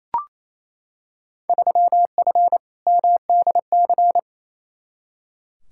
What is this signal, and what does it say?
Morse code, heard as a keyed tone, about 28 wpm, 700 Hz. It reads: 3F MDC
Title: Field Day Exchanges Encore - 28wpm